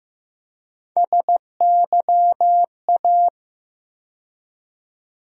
Morse code, heard as a keyed tone, about 15 wpm, 700 Hz. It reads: SYA
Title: USA Call Sign Postfixes - 3 Letters 15wpm